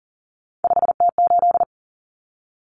Morse code, heard as a keyed tone, about 40 wpm, 700 Hz. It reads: 5T8
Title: Common Cut Numbers - 3-Digit 40wpm